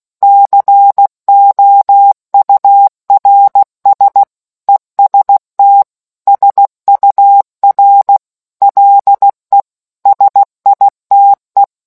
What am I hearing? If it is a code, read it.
Morse code, heard as a keyed tone, about 16 wpm, 800 Hz. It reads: COURS EST SUR LE SITE